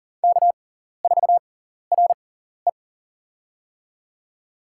Morse code, heard as a keyed tone, about 43 wpm, spaced as for 14 wpm, 690 Hz. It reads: K4RE
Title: USA Call Signs - 4 Characters - 6x Character Spacing - 40wpm